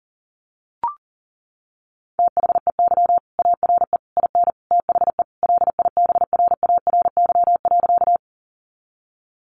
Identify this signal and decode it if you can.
Morse code, heard as a keyed tone, about 40 wpm, 700 Hz. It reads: THEY ARE IN THE LIBRARY.
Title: Sentences from Top 600 Words 40wpm